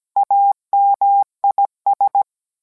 Morse code, heard as a keyed tone, about 17 wpm, 800 Hz. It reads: AMIS